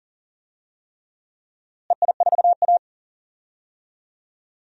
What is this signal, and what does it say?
Morse code, heard as a keyed tone, about 39 wpm, 700 Hz. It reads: EI4A
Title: DXCC-100 Call Signs - 4 Characters - Repeated 3x - 40wpm